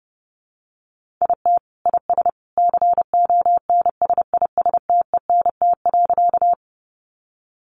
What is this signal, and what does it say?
Morse code, heard as a keyed tone, about 30 wpm, 700 Hz. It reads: IT IS CONSISTENT.
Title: Sentences from Top 1500 Words 30wpm